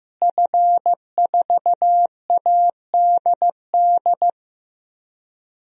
Morse code, heard as a keyed tone, about 15 wpm, 700 Hz. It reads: F4ADD
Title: DXCC-100 Call Signs - 5 Characters - Repeated 2x - 15wpm